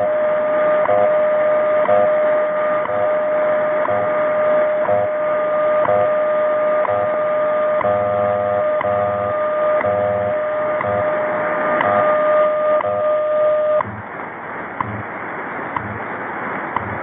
Radio time signal, with noise.
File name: WWV 0830